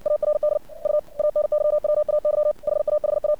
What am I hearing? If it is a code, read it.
Morse code, heard as a keyed tone, about 61 wpm, 600 Hz. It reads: ARD U ANYWAY HAHA
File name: archived / CRSnet